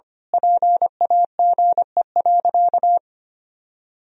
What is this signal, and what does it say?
Morse code, heard as a keyed tone, about 25 wpm, 700 Hz. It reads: PAGE.
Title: Sentences from Top 1000 Words - Review Part 1 - 25wpm